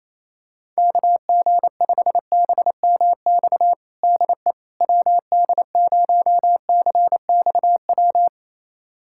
Morse code, heard as a keyed tone, about 28 wpm, 700 Hz. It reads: KG5BMX DE WD0CXW
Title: USA Callsign DE Callsign - 6 Characters - 28wpm